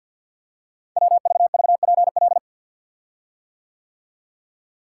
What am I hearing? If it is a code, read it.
Morse code, heard as a keyed tone, about 52 wpm, 700 Hz. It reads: WVVPL
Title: Code Groups - 5 Letter:Numbers - 50wpm